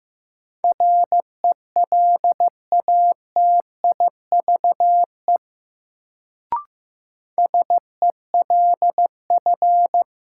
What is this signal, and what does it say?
Morse code, heard as a keyed tone, about 15 wpm, 700 Hz. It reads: RELATIVE SELF
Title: Sentences from Top 1400 Words - New Word Drill Alt - 15wpm